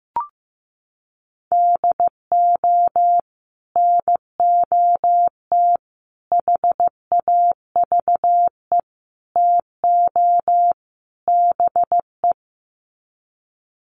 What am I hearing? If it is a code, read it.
Morse code, heard as a keyed tone, about 15 wpm, 700 Hz. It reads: DO NOT HAVE TO BE
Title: Sets of 5 Words 15wpm